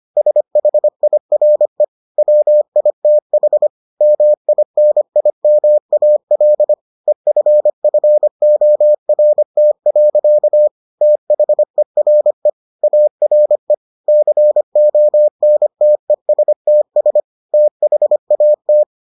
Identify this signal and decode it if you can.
Morse code, heard as a keyed tone, about 25 wpm, 600 Hz. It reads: SHIRE WITH MINIMAL EFFORT. THERE ARE CONTESTS THAT